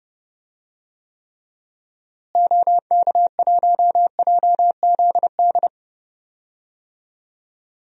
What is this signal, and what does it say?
Morse code, heard as a keyed tone, about 30 wpm, 700 Hz. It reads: OK1JZD
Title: DXCC-100 Call Signs - 6 Characters - Repeated 2x - 30wpm